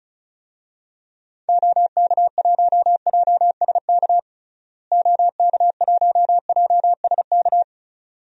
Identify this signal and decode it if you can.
Morse code, heard as a keyed tone, about 36 wpm, 700 Hz. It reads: OK1JSK OK1JSK
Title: DXCC-100 Call Signs - 6 Characters - Repeated 2x - 35wpm